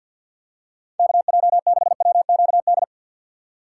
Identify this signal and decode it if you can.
Morse code, heard as a keyed tone, about 52 wpm, 700 Hz. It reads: KJ6WQD
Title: USA Call Signs - 3x Character Spacing - 50wpm